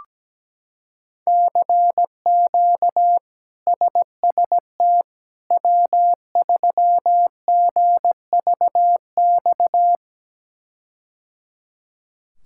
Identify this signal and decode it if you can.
Morse code, heard as a keyed tone, about 17 wpm, 700 Hz. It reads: CQ SST W3GVX
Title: K1USN SST QSO Practice - 17wpm